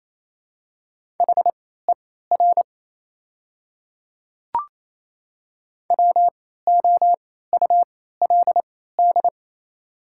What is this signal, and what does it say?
Morse code, heard as a keyed tone, about 27 wpm, spaced as for 16 wpm, 700 Hz. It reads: HER WOULD
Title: 3x Character Spacing - Top 300 Words - 28wpm